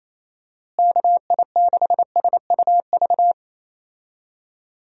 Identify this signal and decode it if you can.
Morse code, heard as a keyed tone, about 28 wpm, 700 Hz. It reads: KI6SUV
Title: USA Call Signs - 6 Characters - Repeated 2x - 28wpm